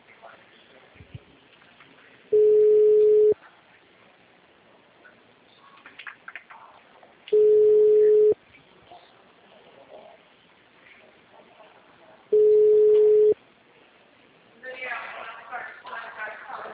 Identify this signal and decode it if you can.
Ringback tone (Europe, 425 Hz), with speech.